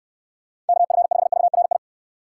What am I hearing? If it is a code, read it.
Morse code, heard as a keyed tone, about 69 wpm, 700 Hz. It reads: KF5VRI